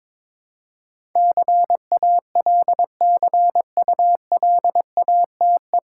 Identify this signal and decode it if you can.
Morse code, heard as a keyed tone, about 21 wpm, 700 Hz. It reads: CALCULATE
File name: 9 Letter Words - First 3 Letters Given - 22wpm